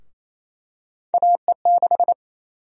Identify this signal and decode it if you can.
Morse code, heard as a keyed tone, about 27 wpm, 700 Hz. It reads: AE6
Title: USA Call Sign Prefixes - 28wpm